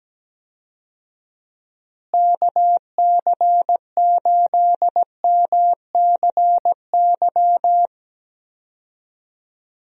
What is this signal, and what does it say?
Morse code, heard as a keyed tone, about 17 wpm, 700 Hz. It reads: KC8MCY
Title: USA Call Signs - 6 Characters - Repeated 3x - 17wpm